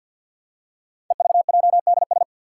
Morse code, heard as a keyed tone, about 45 wpm, 700 Hz. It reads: EVJDS
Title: Code Groups - 5 Letters - 50wpm